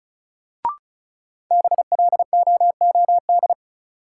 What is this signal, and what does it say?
Morse code, heard as a keyed tone, about 34 wpm, 700 Hz. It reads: BLOOD
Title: Sentences from Top 1100 Words - New Word Drill Alt - 35wpm